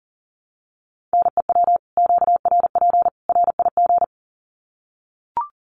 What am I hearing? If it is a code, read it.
Morse code, heard as a keyed tone, about 40 wpm, 700 Hz. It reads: NEW QRP RIG
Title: Common QSO Phrases 40wpm